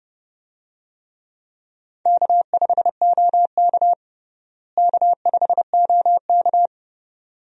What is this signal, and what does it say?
Morse code, heard as a keyed tone, about 30 wpm, 700 Hz. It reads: K5OK K5OK
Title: USA Call Signs - 4 Characters - Repeated 2x - 30wpm